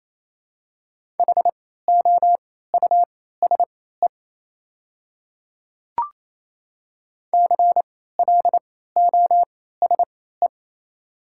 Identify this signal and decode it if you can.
Morse code, heard as a keyed tone, about 28 wpm, spaced as for 16 wpm, 700 Hz. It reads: HOUSE CLOSE
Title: 3x Character Spacing - Top 300 Words - 28wpm